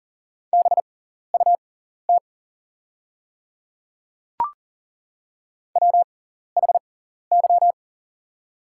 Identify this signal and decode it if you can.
Morse code, heard as a keyed tone, about 40 wpm, spaced as for 14 wpm, 690 Hz. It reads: BUT WHY
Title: 6x Character Spacing - Top 200 Words - 40wpm